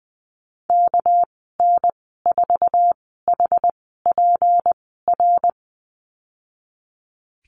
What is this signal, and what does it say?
Morse code, heard as a keyed tone, about 20 wpm, spaced as for 14 wpm, 700 Hz. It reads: KN4HPR
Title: 245 Lesson 39g - Letter z - Call Signs 20wpm